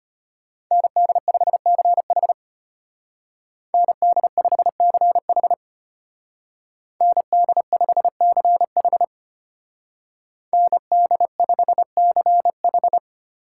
Morse code, heard as a keyed tone, about 32 wpm, 700 Hz. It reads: ND5CH ND5CH ND5CH ND5CH